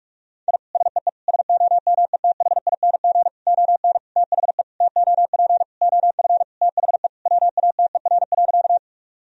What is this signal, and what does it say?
Morse code, heard as a keyed tone, about 46 wpm, 700 Hz. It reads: I SEE SOMETHING ON THE TOP OF THE WATER.
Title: Sentences from Top 1000 Words - Review Part 2 - 45wpm